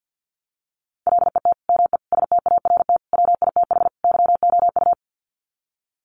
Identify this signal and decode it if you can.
Morse code, heard as a keyed tone, about 48 wpm, 700 Hz. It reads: LET ME START WITH YOU
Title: Sets of 5 Words Encore 50wpm